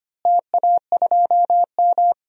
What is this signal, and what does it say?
Morse code, heard as a keyed tone, about 25 wpm, 700 Hz. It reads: TA2M
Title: DXCC-100 Call Signs - 4 Characters - Repeated 3x - 25wpm